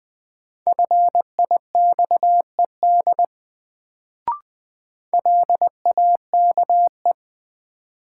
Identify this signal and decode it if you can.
Morse code, heard as a keyed tone, about 20 wpm, 700 Hz. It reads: FIXED LAKE
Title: Sentences from Top 1100 Words - New Word Drill Alt - 20wpm